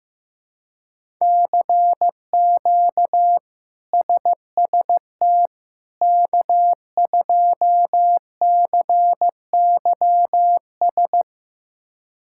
Morse code, heard as a keyed tone, about 15 wpm, 700 Hz. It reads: CQ SST K2CYS
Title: K1USN SST QSO Practice - 15wpm